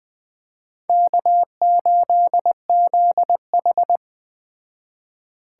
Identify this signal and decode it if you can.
Morse code, heard as a keyed tone, about 20 wpm, 700 Hz. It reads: K8ZH
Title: Speed Racing - USA 1x2 Call Signs 17wpm